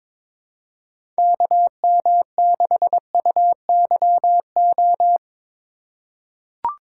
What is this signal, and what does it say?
Morse code, heard as a keyed tone, about 22 wpm, 700 Hz. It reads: KM6UYO
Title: USA Call Signs - 2x Character Spacing - 22wpm